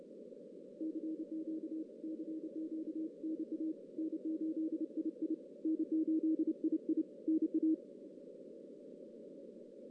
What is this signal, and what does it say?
Morse code, heard as a keyed tone, about 29 wpm, 320 Hz. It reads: CQ CQ NA N8II N8II NA
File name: 2016 JAN CW NAQP / 2016 Jan naqp k5ot - 0-3